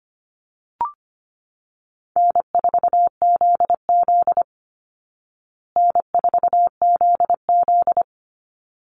Morse code, heard as a keyed tone, about 25 wpm, 700 Hz. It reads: N4ZZ N4ZZ
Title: Easy Call Signs Repeated 2x - 25wpm